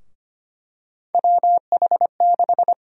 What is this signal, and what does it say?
Morse code, heard as a keyed tone, about 25 wpm, 700 Hz. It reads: WH6
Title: USA Call Sign Prefixes - 25wpm